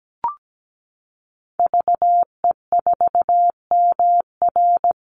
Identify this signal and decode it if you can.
Morse code, heard as a keyed tone, about 17 wpm, 700 Hz. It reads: VE4MR
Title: Call Signs Repeated 2x - 17wpm